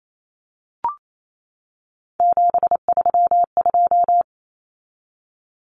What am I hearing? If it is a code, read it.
Morse code, heard as a keyed tone, about 28 wpm, 700 Hz. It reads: 732
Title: Numbers - 3-Digit 28wpm